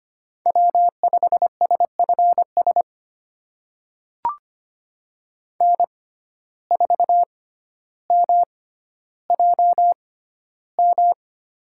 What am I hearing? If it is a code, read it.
Morse code, heard as a keyed tone, about 25 wpm, spaced as for 13 wpm, 700 Hz. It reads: W5SFS N4MJM
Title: USA Call Signs - 6x Character Spacing - 25wpm